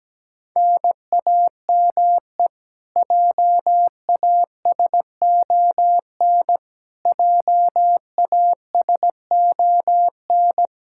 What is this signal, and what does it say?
Morse code, heard as a keyed tone, about 17 wpm, 700 Hz. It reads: NAME JASON JASON
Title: Practice QSO - Exchange 1 - Beginning - 17wpm